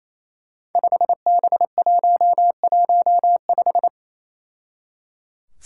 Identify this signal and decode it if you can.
Morse code, heard as a keyed tone, about 28 wpm, 700 Hz. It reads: 5B115
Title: Five Challenging Letter-Number Combinations 28wpm